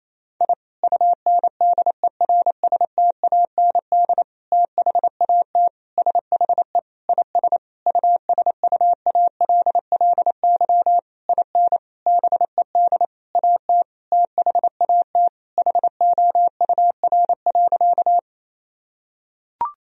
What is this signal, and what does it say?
Morse code, heard as a keyed tone, about 28 wpm, 700 Hz. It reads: I UNDERSTAND THAT SHE IS USUALLY IN BED AT THAT HOUR.
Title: Sentences from Top 1000 Words - Review Part 3 - 28wpm